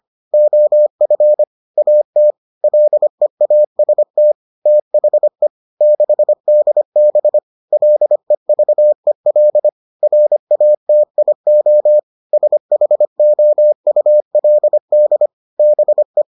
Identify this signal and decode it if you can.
Morse code, heard as a keyed tone, about 25 wpm, 600 Hz. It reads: OF AT LEAST THE 6DB LEVEL RATIO SHOULD BE